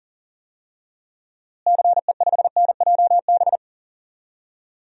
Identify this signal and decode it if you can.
Morse code, heard as a keyed tone, about 37 wpm, 700 Hz. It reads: CE5NJB
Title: DXCC-100 Call Signs - 6 Characters - Repeated 3x - 40wpm